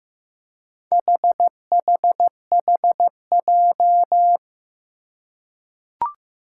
Morse code, heard as a keyed tone, about 15 wpm, 700 Hz. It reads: HHHJ
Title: Four Challenging Letter-Number Combinations 15wpm